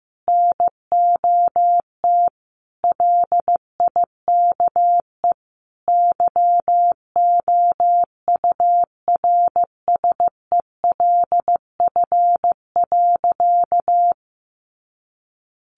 Morse code, heard as a keyed tone, about 15 wpm, 700 Hz. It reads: NOT LIKE YOURSELF.
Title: Sentences from Top 600 Words 15wpm